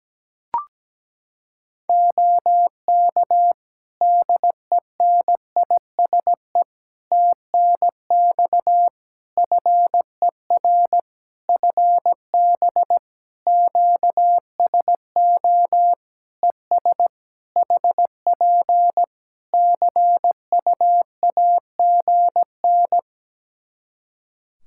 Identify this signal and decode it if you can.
Morse code, heard as a keyed tone, about 17 wpm, 700 Hz. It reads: OK DENISE TNX FER FB QSO ES HP CUAGN
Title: Practice QSO - Exchange 4 - Ending - 17wpm